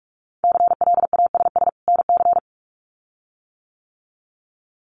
Morse code, heard as a keyed tone, about 46 wpm, 700 Hz. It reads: CLASS NC
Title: Winter Field Day Exchanges 45wpm